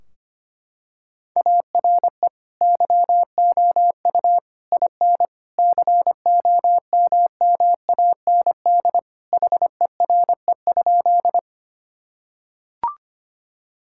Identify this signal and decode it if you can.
Morse code, heard as a keyed tone, about 25 wpm, 700 Hz. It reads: ARE YOU IN COMMAND HERE?
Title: Sentences from Top 1600 Words 25wpm